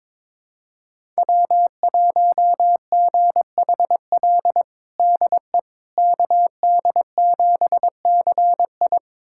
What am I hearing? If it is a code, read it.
Morse code, heard as a keyed tone, about 22 wpm, 700 Hz. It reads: W1GHL DE KD7CI
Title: USA Callsign DE Callsign - 5 Characters - 22wpm